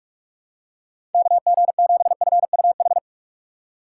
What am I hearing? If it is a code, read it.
Morse code, heard as a keyed tone, about 46 wpm, 700 Hz. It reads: KG7FUH
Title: USA Call Signs - 6 Characters - 6x Character Spacing - 45wpm